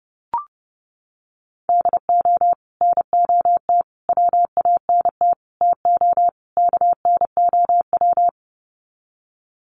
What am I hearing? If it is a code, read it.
Morse code, heard as a keyed tone, about 30 wpm, 700 Hz. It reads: DO NOT WANT TO KNOW
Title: Sets of 5 Words Encore 30wpm